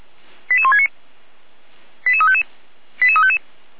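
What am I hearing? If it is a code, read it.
Five-tone selective call (ZVEI), heard as quick runs of five tones, tones of about 70 ms. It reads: 88169 88355 88355
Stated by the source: ZVEI five-tone selective call, code 88191, received with an SDR.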